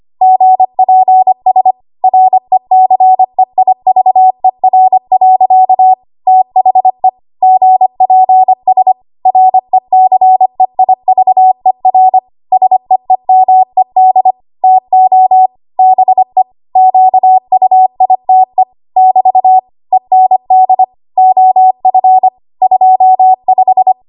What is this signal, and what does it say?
Morse code, heard as a keyed tone, about 25 wpm, 750 Hz. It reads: GPS RECEIVER. THE GPS RECEIVER SEEMED TO BE QUITE <BT> END OF 25
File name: Archive / 25 WPM / 25WPM